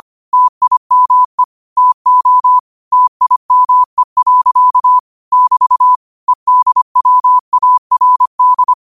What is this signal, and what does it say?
Morse code, heard as a keyed tone, about 25 wpm, 1000 Hz. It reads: TIME TO TIME. <BT> EDWARD
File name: qotdfiles / 25wpm